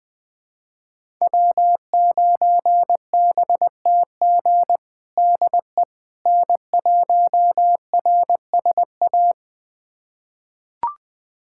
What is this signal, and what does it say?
Morse code, heard as a keyed tone, about 21 wpm, 700 Hz. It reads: W9BTG DE N1RSA
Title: USA Callsign DE Callsign - 5 Characters - 20wpm